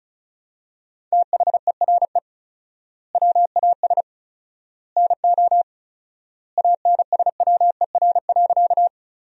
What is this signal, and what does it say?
Morse code, heard as a keyed tone, about 34 wpm, 700 Hz. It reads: THERE WAS NO ANSWER.
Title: 4x Word-Spacing - Sentences from Top 300 Words - 35wpm